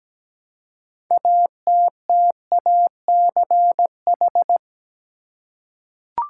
Morse code, heard as a keyed tone, about 17 wpm, 700 Hz. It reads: ATTACH
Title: Six Letter Words - 17wpm